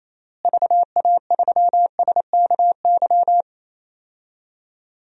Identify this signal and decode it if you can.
Morse code, heard as a keyed tone, about 28 wpm, 700 Hz. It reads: VA3SKY
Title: DXCC-100 Call Signs - 6 Characters - Repeated 3x - 28wpm